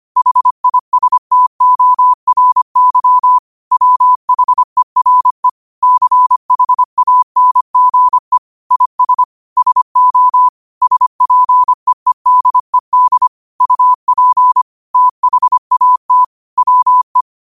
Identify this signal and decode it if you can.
Morse code, heard as a keyed tone, about 25 wpm, 1000 Hz. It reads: HISTORY WHERE CHANGE IS SO SPEEDED UP THAT WE
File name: qotdfiles / 25wpm